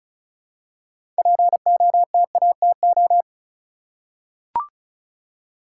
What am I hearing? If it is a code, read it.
Morse code, heard as a keyed tone, about 36 wpm, 700 Hz. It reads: POTATO
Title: Six Letter Words - 35wpm